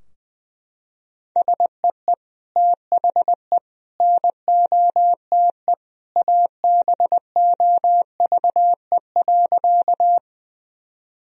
Morse code, heard as a keyed tone, about 21 wpm, 700 Hz. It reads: SEE THE NOTE ABOVE.
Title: Sentences from Top 1000 Words - Review Part 2 - 20wpm